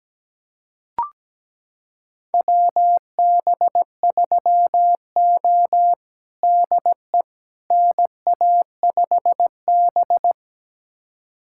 Morse code, heard as a keyed tone, about 17 wpm, 700 Hz. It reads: WB3O DE NA5B
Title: USA Callsign DE Callsign - 4 Characters - 17wpm